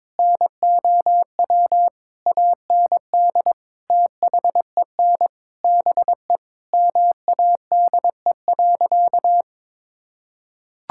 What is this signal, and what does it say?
Morse code, heard as a keyed tone, about 22 wpm, 700 Hz. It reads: NOW AND THEN BE MADE.
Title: Sentences from Top 1600 Words 22wpm